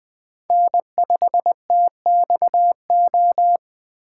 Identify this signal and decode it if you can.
Morse code, heard as a keyed tone, about 20 wpm, 700 Hz. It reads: N5TXO
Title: USA Call Signs - 5 Characters - 4x Character Spacing - 20wpm